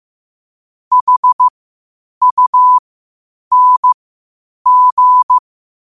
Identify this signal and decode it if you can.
Morse code, heard as a keyed tone, about 15 wpm, spaced as for 8 wpm, 1000 Hz. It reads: HUNG
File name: qotdfiles / 8wpm